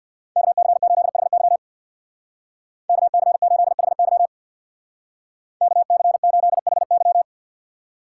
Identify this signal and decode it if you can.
Morse code, heard as a keyed tone, about 62 wpm, 700 Hz. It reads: KC8HY KC8HY KC8HY
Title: Speed Racing - USA 2x2 Call Signs 50wpm